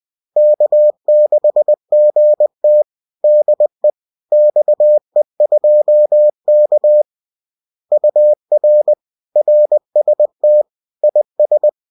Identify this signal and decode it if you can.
Morse code, heard as a keyed tone, about 20 wpm, 600 Hz. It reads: K6GT DE XE2K UR RST IS